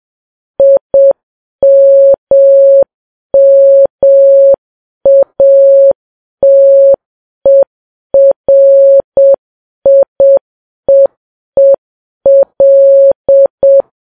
Morse code, heard as a keyed tone, about 7 wpm, 550 Hz. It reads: IMMATERIEEL
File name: Files Web tls / Morse on9fv / morse7kps